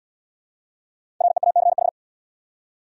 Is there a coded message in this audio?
Yes: Morse code from 1.2 to 1.9 s, heard as a keyed tone, about 66 wpm, 700 Hz. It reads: REACH